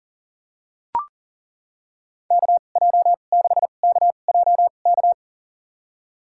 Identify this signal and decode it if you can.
Morse code, heard as a keyed tone, about 42 wpm, spaced as for 29 wpm, 690 Hz. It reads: KJ6KJK
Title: USA Call Signs - 6 Characters - 2x Character Spacing - 40wpm